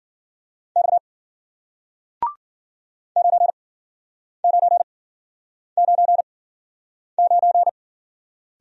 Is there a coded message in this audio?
Yes: Morse code from 0.8 to 7.7 s, heard as a keyed tone, about 55 wpm, spaced as for 9 wpm, 700 Hz. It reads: X 9999